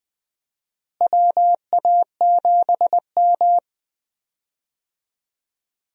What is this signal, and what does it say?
Morse code, heard as a keyed tone, about 20 wpm, 700 Hz. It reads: WA7M
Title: USA Call Signs - 4 Characters - Repeated 2x - 20wpm